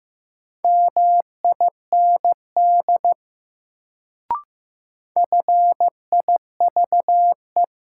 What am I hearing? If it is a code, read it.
Morse code, heard as a keyed tone, about 15 wpm, 700 Hz. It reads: MIND FIVE
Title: Sets of 5 Words - New Word Drill - 15wpm